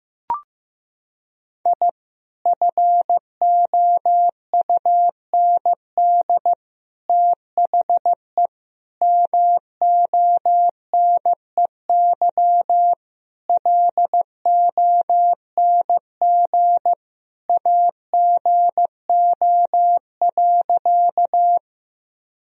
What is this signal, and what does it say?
Morse code, heard as a keyed tone, about 15 wpm, 700 Hz. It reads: I FOUND THE MONEY LONG AGO.
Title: Sentences from Top 1000 Words - Review Part 3 - 15wpm